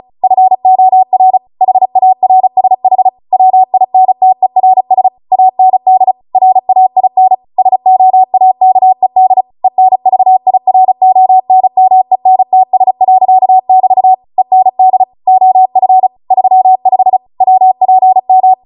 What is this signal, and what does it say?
Morse code, heard as a keyed tone, about 36 wpm, 750 Hz. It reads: FOR HARSH WINTERS AND RAIN SOAKED ENVIRONMENTS.<BT> END OF 35 WPM